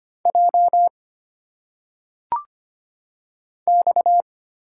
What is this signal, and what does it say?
Morse code, heard as a keyed tone, about 25 wpm, 700 Hz. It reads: J X
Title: Single Letters - Phonetic Pronunciation 25wpm